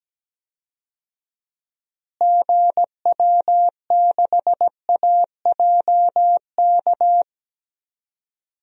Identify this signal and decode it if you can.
Morse code, heard as a keyed tone, about 17 wpm, 700 Hz. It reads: GW6AJK
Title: DXCC-100 Call Signs - 6 Characters - Repeated 2x - 17wpm